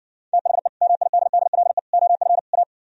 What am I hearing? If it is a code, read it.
Morse code, heard as a keyed tone, about 69 wpm, 700 Hz. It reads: THE MIDDLE OF A